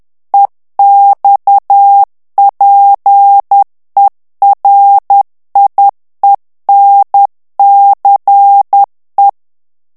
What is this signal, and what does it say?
Morse code, heard as a keyed tone, about 10 wpm, 800 Hz. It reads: EXPERIENCE
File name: morse mp3 / ANGLAIS texte clair 2 vitesse 10 mots minute